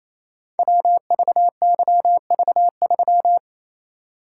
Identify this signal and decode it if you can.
Morse code, heard as a keyed tone, about 27 wpm, 700 Hz. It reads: WVYV3
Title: Code Groups - 5 Letter:Numbers - 28wpm